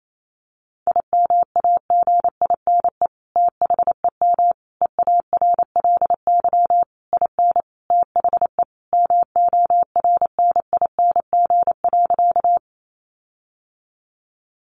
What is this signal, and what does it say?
Morse code, heard as a keyed tone, about 28 wpm, 700 Hz. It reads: IMAGINE THEM EARLY IN THE MORNING.
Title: Sentences from Top 1400 Words 28wpm